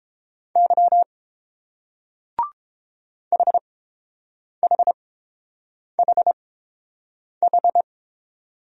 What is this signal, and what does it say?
Morse code, heard as a keyed tone, about 30 wpm, spaced as for 7 wpm, 700 Hz. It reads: Y HHHH